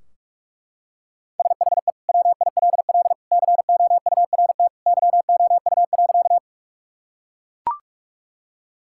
Morse code, heard as a keyed tone, about 42 wpm, 700 Hz. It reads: SHE WILL COURT YOU.
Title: Sentences from Top 1600 Words 45wpm